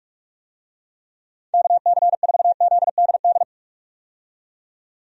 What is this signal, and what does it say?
Morse code, heard as a keyed tone, about 46 wpm, 700 Hz. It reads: KC4ZDD
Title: USA Call Signs - 6 Characters - 45wpm